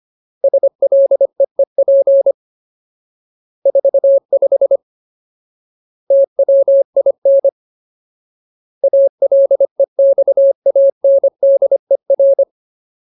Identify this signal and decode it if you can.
Morse code, heard as a keyed tone, about 25 wpm, 550 Hz. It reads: SLEEP 45 TWIN ALEXANDER
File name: Ultimate Word List / 25WPM Ultimate